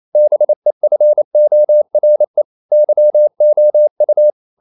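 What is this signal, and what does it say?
Morse code, heard as a keyed tone, about 28 wpm, 600 Hz. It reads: BEFORE YOU
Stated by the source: Morse code practice at 28 wpm, text of Ken Follett's novel Never in English, from CW med Gnister.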